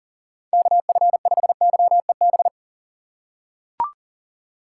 Morse code, heard as a keyed tone, about 40 wpm, 700 Hz. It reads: KF5YEB
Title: USA Call Signs - 6 Characters - 2x Character Spacing - 40wpm